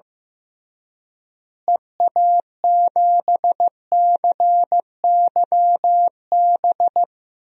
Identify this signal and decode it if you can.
Morse code, heard as a keyed tone, about 16 wpm, 700 Hz. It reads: EA7CYB
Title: DXCC-100 Call Signs - 6 Characters - Repeated 3x - 15wpm